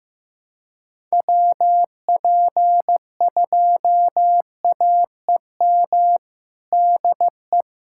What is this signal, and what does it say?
Morse code, heard as a keyed tone, about 16 wpm, 700 Hz. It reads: WP2AEM DE
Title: USA Callsign DE Callsign - 6 Characters - 15wpm